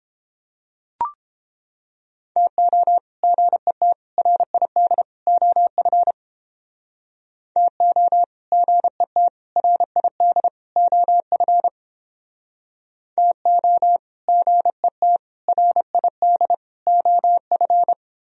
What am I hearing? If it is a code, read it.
Morse code, heard as a keyed tone, about 31 wpm, 700 Hz. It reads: TO GET RID OF TO GET RID OF TO GET RID OF